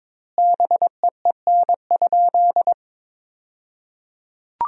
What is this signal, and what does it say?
Morse code, heard as a keyed tone, about 22 wpm, 700 Hz. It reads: BEEN?
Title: 2x Word-Spacing - Sentences from Top 400 Words - 22wpm